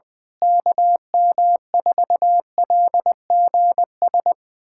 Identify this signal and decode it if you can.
Morse code, heard as a keyed tone, about 21 wpm, 700 Hz. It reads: KM4LGS
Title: USA Callsign DE Callsign 20wpm